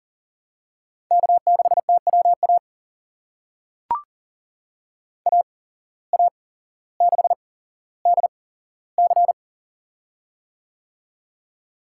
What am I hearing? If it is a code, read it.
Morse code, heard as a keyed tone, about 40 wpm, spaced as for 17 wpm, 700 Hz. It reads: K6TWA AA6DC
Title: USA Call Signs - 8x Character Spacing - 40wpm